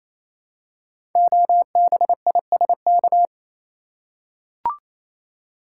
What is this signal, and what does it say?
Morse code, heard as a keyed tone, about 29 wpm, 700 Hz. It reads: OBISK
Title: Code Groups - 5 Letters - 28wpm